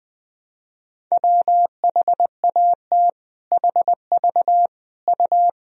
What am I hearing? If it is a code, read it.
Morse code, heard as a keyed tone, about 21 wpm, 700 Hz. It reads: WHAT HV U
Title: Sentences from Top 400 Words and Abbreviations 20wpm